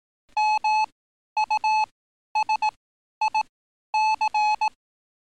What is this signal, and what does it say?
Morse code, heard as a keyed tone, about 18 wpm, spaced as for 11 wpm, 860 Hz. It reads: MUSIC
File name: training / NAVPERS 11049to11060 / NAVPERS 11051 RA RB